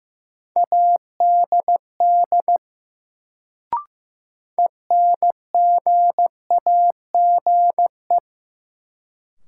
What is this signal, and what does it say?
Morse code, heard as a keyed tone, about 15 wpm, 700 Hz. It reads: ADD ENGAGE
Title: Sentences from Top 1500 Words - New Word Drill Alt - 15wpm